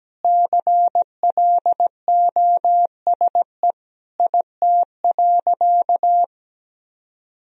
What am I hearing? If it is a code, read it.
Morse code, heard as a keyed tone, about 17 wpm, 700 Hz. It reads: CLOSE IT.
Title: Sentences from Top 1000 Words - Review Part 1 - 17wpm